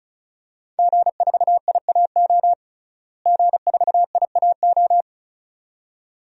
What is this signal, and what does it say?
Morse code, heard as a keyed tone, about 36 wpm, 700 Hz. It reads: G4IAO G4IAO
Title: DXCC-100 Call Signs - 5 Characters - Repeated 2x - 35wpm